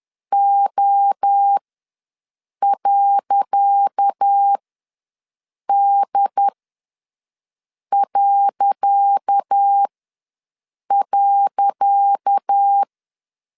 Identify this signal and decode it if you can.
Morse code, heard as a keyed tone, about 10 wpm, spaced as for 5 wpm, 800 Hz. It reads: O.D..
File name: LEZIONI / 10 SIGNS / 10C DOT ws 10wpm 4213